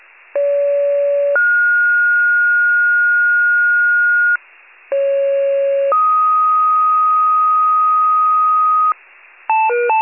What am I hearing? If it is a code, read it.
Two-tone page, heard as a one-second tone, then a three-second tone, about 570 then 1430 Hz, about 550 then 1250 Hz, with noise. It is heard from 0.3 to 8.9 s.
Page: 29 FIRE